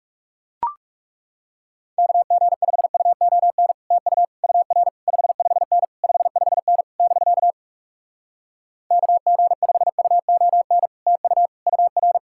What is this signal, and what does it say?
Morse code, heard as a keyed tone, about 40 wpm, 700 Hz. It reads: KG5UON TU UR 55N 55N <BK> KG5UON TU UR